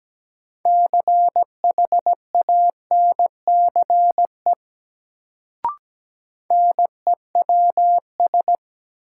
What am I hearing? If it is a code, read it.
Morse code, heard as a keyed tone, about 17 wpm, 700 Hz. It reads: CHANCE NEWS
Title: Sentences from Top 800 Words - New Word Drill - 17wpm